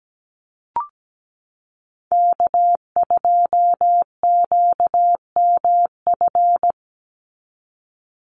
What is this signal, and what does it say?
Morse code, heard as a keyed tone, about 17 wpm, 700 Hz. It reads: K2QMF
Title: Call Signs Repeated 2x - 17wpm